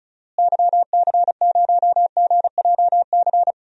Morse code, heard as a keyed tone, about 36 wpm, 700 Hz. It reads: YC0GJC
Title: DXCC-100 Call Signs - 6 Characters - Repeated 3x - 35wpm